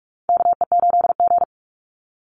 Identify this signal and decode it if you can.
Morse code, heard as a keyed tone, about 44 wpm, 700 Hz. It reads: KE8G
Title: Call Signs Repeated 3x - 45wpm